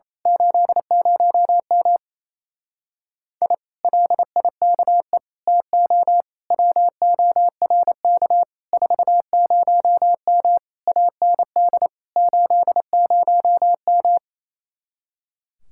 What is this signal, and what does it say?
Morse code, heard as a keyed tone, about 27 wpm, 700 Hz. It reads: 80M I LIKE TO WORK 40M AND 80M
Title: Speed-Racing Common QSO Phrases - 28wpm